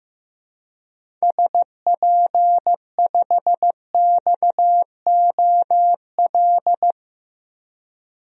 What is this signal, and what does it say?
Morse code, heard as a keyed tone, about 15 wpm, 700 Hz. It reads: SP5XOL
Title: DXCC-100 Call Signs - 6 Characters - Repeated 3x - 15wpm